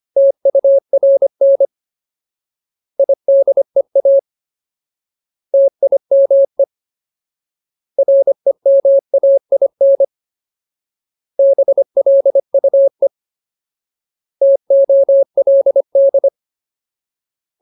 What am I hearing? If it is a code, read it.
Morse code, heard as a keyed tone, about 25 wpm, 550 Hz. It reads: TURN IDEA TIME REMAIN BLUE TOLD
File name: Ultimate Word List / 25WPM Ultimate